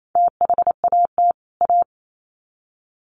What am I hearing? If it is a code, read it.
Morse code, heard as a keyed tone, about 28 wpm, 700 Hz. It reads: THAT A
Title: Sets of 2 Words 28wpm